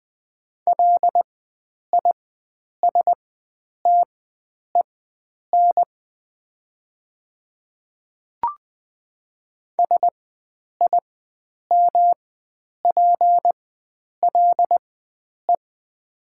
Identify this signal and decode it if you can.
Morse code, heard as a keyed tone, about 20 wpm, spaced as for 9 wpm, 700 Hz. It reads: LISTEN SIMPLE
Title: 4x Character Spacing - Top 500 Words - 20wpm